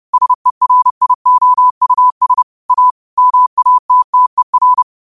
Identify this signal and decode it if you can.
Morse code, heard as a keyed tone, about 30 wpm, 1000 Hz. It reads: SERIOUS A MATTER